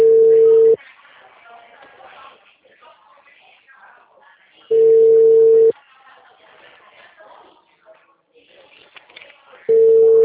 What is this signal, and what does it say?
Ringback tone (China, 450 Hz), with speech.